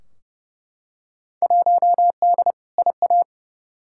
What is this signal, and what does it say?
Morse code, heard as a keyed tone, about 30 wpm, 700 Hz. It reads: 1D IA
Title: Field Day Exchanges - 30wpm